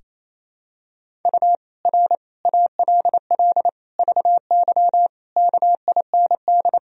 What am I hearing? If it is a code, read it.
Morse code, heard as a keyed tone, about 28 wpm, 700 Hz. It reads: U R ALL VY KIND
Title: Sentences from Top 200 Words and Abbreviations 28wpm